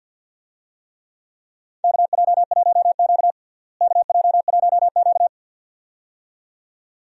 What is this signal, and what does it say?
Morse code, heard as a keyed tone, about 50 wpm, 700 Hz. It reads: KJ1Q KJ1Q
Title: USA Call Signs - 4 Characters - Repeated 2x - 50wpm